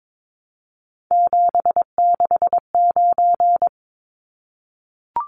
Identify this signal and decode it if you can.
Morse code, heard as a keyed tone, about 22 wpm, 700 Hz. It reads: 769
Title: Numbers - 3-Digit 22wpm